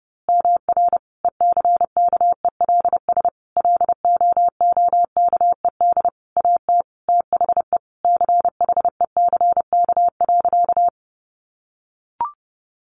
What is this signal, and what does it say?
Morse code, heard as a keyed tone, about 30 wpm, 700 Hz. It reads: MR ECKELS LOOKED AT THE CHECK.
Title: A Sound Of Thunder 30wpm